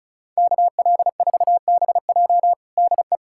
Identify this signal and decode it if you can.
Morse code, heard as a keyed tone, about 36 wpm, 700 Hz. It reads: KL4BJ DE
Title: USA Callsign DE Callsign 35wpm